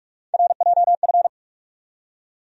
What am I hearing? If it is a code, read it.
Morse code, heard as a keyed tone, about 44 wpm, 700 Hz. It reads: RJF
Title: USA Call Sign Postfixes - 3 Letters 45wpm